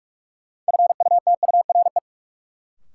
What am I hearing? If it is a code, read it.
Morse code, heard as a keyed tone, about 46 wpm, 700 Hz. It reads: FUTURE
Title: Sets of 3 Words - New Word Drill - 45wpm